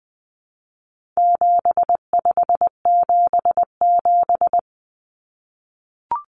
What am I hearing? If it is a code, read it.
Morse code, heard as a keyed tone, about 20 wpm, 700 Hz. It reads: 7577
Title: Common Cut Numbers - 4-Digit 20wpm